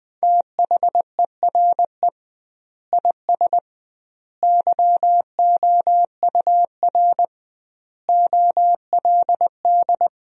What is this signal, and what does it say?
Morse code, heard as a keyed tone, about 21 wpm, 700 Hz. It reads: THERE IS YOUR OLD
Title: 2x Word-Spacing - Sentences from Top 500 Words - 20wpm